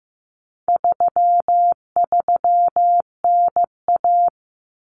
Morse code, heard as a keyed tone, about 15 wpm, 700 Hz. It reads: 33NA
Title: Common Cut Numbers - 4-Digit 15wpm